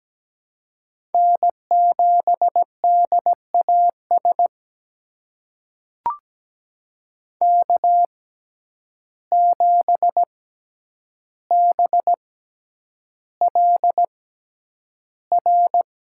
Morse code, heard as a keyed tone, about 17 wpm, spaced as for 9 wpm, 700 Hz. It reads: N7DAS K7BLR
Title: USA Call Signs - 5 Characters - 6x Character Spacing - 17wpm